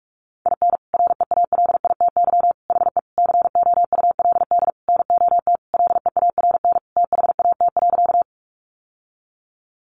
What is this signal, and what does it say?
Morse code, heard as a keyed tone, about 44 wpm, 700 Hz. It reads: IN REALITY HE COULD NOT LEARN THAT.
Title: Sentences from Top 1200 Words 45wpm